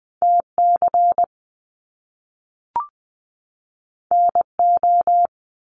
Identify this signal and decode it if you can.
Morse code, heard as a keyed tone, about 20 wpm, 700 Hz. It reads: TC NO
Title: Top Two Letter Combinations in Words 20wpm